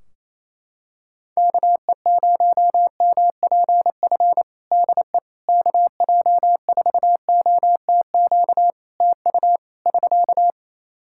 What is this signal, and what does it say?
Morse code, heard as a keyed tone, about 28 wpm, 700 Hz. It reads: KE0MPF DE KJ4OTQ TU <SK>
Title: Practice QSO - Exchange 4 - Ending - 28wpm